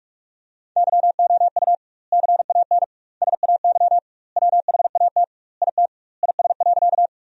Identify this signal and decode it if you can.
Morse code, heard as a keyed tone, about 46 wpm, 700 Hz. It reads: YOU CAN SAY WHAT IT IS.
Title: 2x Word-Spacing - Sentences from Top 200 Words - 45wpm